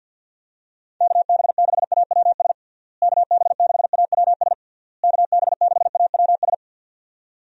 Morse code, heard as a keyed tone, about 46 wpm, 700 Hz. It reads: KB6AWS KB6AWS KB6AWS
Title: USA Call Signs - 6 Characters - Repeated 3x - 50wpm